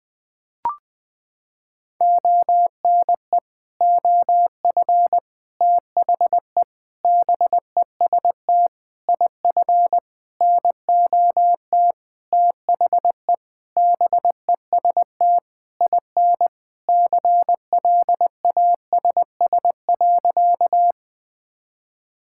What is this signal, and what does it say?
Morse code, heard as a keyed tone, about 20 wpm, 700 Hz. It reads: ONE OF THE BEST IF NOT THE BEST IN CLASS.
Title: Sentences with 8-Word Phrases - 20wpm